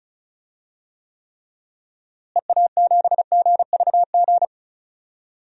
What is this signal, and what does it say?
Morse code, heard as a keyed tone, about 36 wpm, 700 Hz. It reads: EA7GVG
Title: DXCC-100 Call Signs - 6 Characters - Repeated 3x - 35wpm